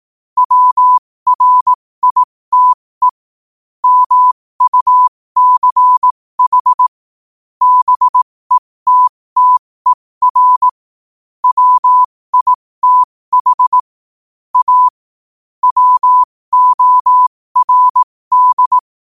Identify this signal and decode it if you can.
Morse code, heard as a keyed tone, about 18 wpm, spaced as for 15 wpm, 1000 Hz. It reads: WRITE MUCH BETTER WITH A WORD
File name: qotdfiles / 15wpm